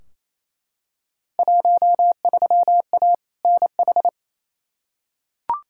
Morse code, heard as a keyed tone, about 28 wpm, 700 Hz. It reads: 13A NH
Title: Field Day Exchanges Encore - 28wpm